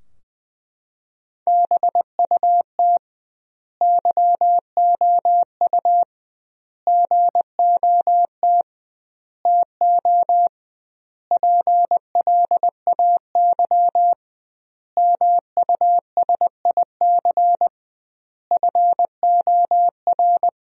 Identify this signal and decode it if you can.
Morse code, heard as a keyed tone, about 20 wpm, 700 Hz. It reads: BUT YOU GOT TO PLAY MUSIC FOR
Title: 2x Word-Spacing - Sentences from Top 300 Words - 20wpm